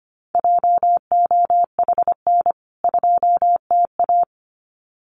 Morse code, heard as a keyed tone, about 25 wpm, 700 Hz. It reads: JOHN 2TA
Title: CWT Exchange 25wpm